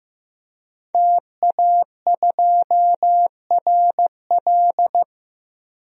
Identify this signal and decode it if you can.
Morse code, heard as a keyed tone, about 15 wpm, 700 Hz. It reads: TA2RL
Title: DXCC-100 Call Signs - 5 Characters - 15wpm